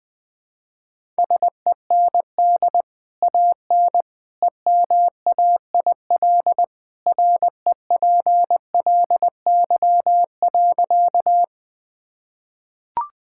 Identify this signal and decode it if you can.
Morse code, heard as a keyed tone, about 20 wpm, 700 Hz. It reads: SEND AN EMAIL REPLY.
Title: Easy Sentences with Hints - 20wpm